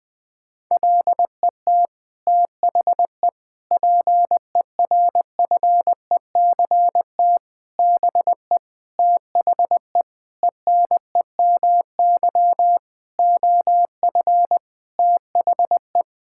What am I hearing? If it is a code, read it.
Morse code, heard as a keyed tone, about 20 wpm, 700 Hz. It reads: LET THE PERFECT BE THE ENEMY OF THE
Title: Sets of 9 Words - 20wpm